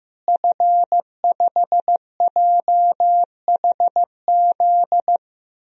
Morse code, heard as a keyed tone, about 15 wpm, 700 Hz. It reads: F5JHZ
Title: DXCC-100 Call Signs - 5 Characters - Repeated 2x - 15wpm